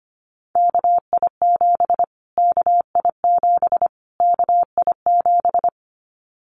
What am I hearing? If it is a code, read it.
Morse code, heard as a keyed tone, about 25 wpm, 700 Hz. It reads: KI7 KI7 KI7
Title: Callsign Prefixes for Top 25 DXCC - Repeated 3x - 25wpm